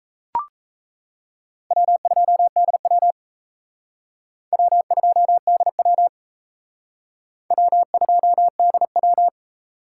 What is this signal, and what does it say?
Morse code, heard as a keyed tone, about 35 wpm, 700 Hz. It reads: W2DW W2DW W2DW
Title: Speed Racing - USA 1x2 Call Signs 28wpm